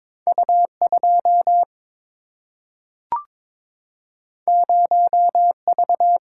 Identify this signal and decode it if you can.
Morse code, heard as a keyed tone, about 22 wpm, 700 Hz. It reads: U2 0V
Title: Single-Letter Single-Number 22wpm